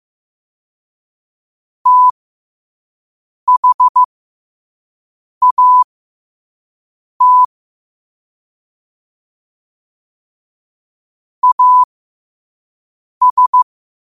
Morse code, heard as a keyed tone, about 15 wpm, spaced as for 5 wpm, 1000 Hz. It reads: THAT AS
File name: qotdfiles / 5wpm